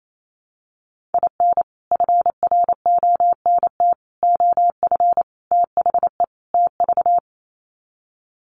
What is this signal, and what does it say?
Morse code, heard as a keyed tone, about 28 wpm, 700 Hz. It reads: IN FRONT OF THE TV
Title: Sets of 5 Words Encore2 28wpm